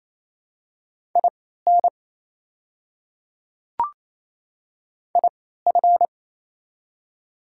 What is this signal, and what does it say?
Morse code, heard as a keyed tone, about 28 wpm, spaced as for 16 wpm, 700 Hz. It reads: IN IF
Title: 3x Character Spacing - Top 200 Words - 28wpm